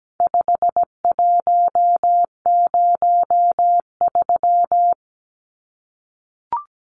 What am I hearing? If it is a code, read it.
Morse code, heard as a keyed tone, about 17 wpm, 700 Hz. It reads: 5103
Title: Numbers - 4-Digit 17wpm